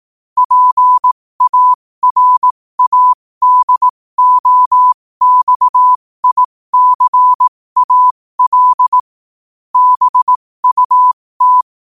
Morse code, heard as a keyed tone, about 18 wpm, spaced as for 15 wpm, 1000 Hz. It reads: PARADOXICAL BUT